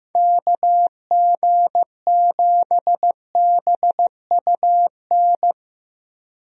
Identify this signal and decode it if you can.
Morse code, heard as a keyed tone, about 15 wpm, 700 Hz. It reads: KG7BUN
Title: USA Call Signs - 6 Characters - Repeated 2x - 15wpm